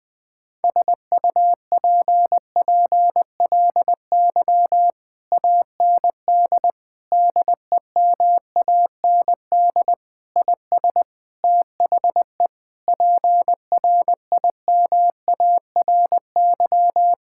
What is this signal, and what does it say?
Morse code, heard as a keyed tone, about 21 wpm, 700 Hz. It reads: SUPPLY AND DEMAND IS THE PRIMARY
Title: Sentences from Top 1000 Words - Review Part 6 - 20wpm